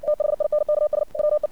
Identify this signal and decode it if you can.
Morse code, heard as a keyed tone, about 56 wpm, 600 Hz. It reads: THINKS WE
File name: archived / CRSnet